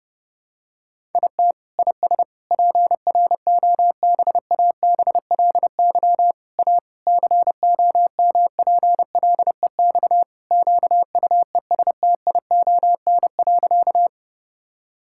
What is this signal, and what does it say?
Morse code, heard as a keyed tone, about 30 wpm, 700 Hz. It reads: IT IS PROBABLY A COMPLEX QUESTION.
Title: Sentences from Top 1000 Words - Review Part 6 - 30wpm